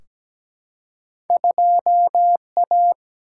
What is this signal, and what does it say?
Morse code, heard as a keyed tone, about 17 wpm, 700 Hz. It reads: 2A
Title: Single-Letter Single-Number 17wpm